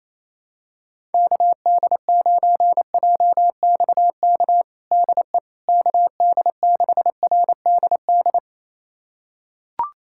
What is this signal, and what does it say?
Morse code, heard as a keyed tone, about 28 wpm, 700 Hz. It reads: KD9JXK DE KD6RDD
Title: USA Callsign DE Callsign - 6 Characters - 28wpm